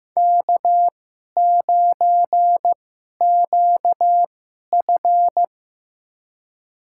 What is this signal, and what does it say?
Morse code, heard as a keyed tone, about 15 wpm, spaced as for 11 wpm, 700 Hz. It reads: K9QF
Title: USA Call Signs - 4 Characters - 2x Character Spacing - 15wpm